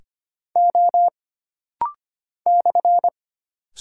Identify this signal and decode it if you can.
Morse code, heard as a keyed tone, about 25 wpm, 700 Hz. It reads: O /
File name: Single Character (Letter:Number:Punctuation:Prosign) - Rapid-Fire - 25wpm